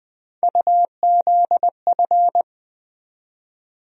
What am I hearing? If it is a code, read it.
Morse code, heard as a keyed tone, about 20 wpm, 700 Hz. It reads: UZF
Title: USA Call Sign Postfixes - 3 Letters 20wpm